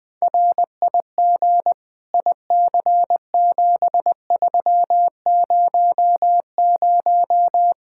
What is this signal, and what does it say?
Morse code, heard as a keyed tone, about 21 wpm, 700 Hz. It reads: RIG IC7300
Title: Practice QSO - Exchange 2 - Info - 20wpm